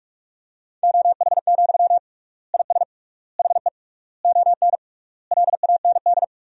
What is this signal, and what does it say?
Morse code, heard as a keyed tone, about 46 wpm, 700 Hz. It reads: OH, IS HE ON LAND
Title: 3x Word-Spacing - Sentences from Top 500 Words - 45wpm